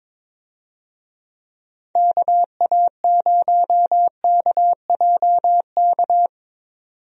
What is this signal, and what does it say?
Morse code, heard as a keyed tone, about 22 wpm, 700 Hz. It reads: KA0KJK
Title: USA Call Signs - 6 Characters - 22wpm